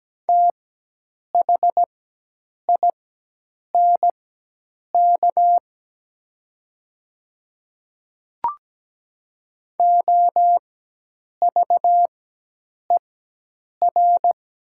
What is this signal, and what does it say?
Morse code, heard as a keyed tone, about 17 wpm, spaced as for 8 wpm, 700 Hz. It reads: THINK OVER
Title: 4x Character Spacing - Top 200 Words - 17wpm